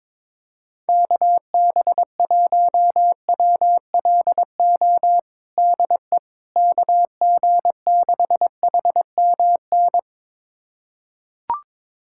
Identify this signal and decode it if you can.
Morse code, heard as a keyed tone, about 22 wpm, 700 Hz. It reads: KB1WLO DE KG6HMN
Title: USA Callsign DE Callsign 22wpm